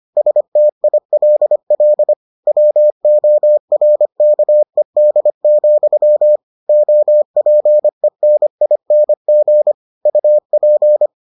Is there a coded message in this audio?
Yes: Morse code from 0.2 to 11.1 s, heard as a keyed tone, about 25 wpm, 600 Hz. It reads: STILL WORKED, OPENING UP